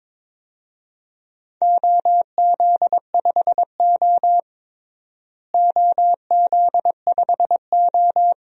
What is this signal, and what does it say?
Morse code, heard as a keyed tone, about 22 wpm, 700 Hz. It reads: OZ5O OZ5O
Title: DXCC-100 Call Signs - 4 Characters - Repeated 3x - 22wpm